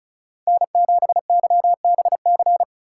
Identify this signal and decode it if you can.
Morse code, heard as a keyed tone, about 34 wpm, 700 Hz. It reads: N7YBC
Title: USA Call Signs - 5 Characters - 4x Character Spacing - 35wpm